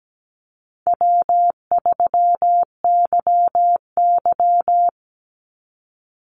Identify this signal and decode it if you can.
Morse code, heard as a keyed tone, about 17 wpm, 700 Hz. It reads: W3YY
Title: Call Signs Repeated 2x - 17wpm